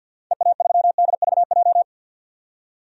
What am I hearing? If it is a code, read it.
Morse code, heard as a keyed tone, about 52 wpm, 700 Hz. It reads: EA3DVJ
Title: DXCC-100 Call Signs - 6 Characters - 50wpm